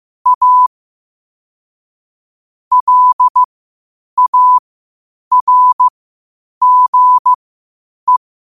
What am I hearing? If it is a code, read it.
Morse code, heard as a keyed tone, about 15 wpm, spaced as for 8 wpm, 1000 Hz. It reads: A LARGE
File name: qotdfiles / 8wpm